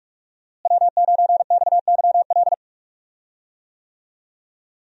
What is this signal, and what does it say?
Morse code, heard as a keyed tone, about 42 wpm, 700 Hz. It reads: W9XYL
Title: USA Call Signs - 5 Characters - Repeated 2x - 45wpm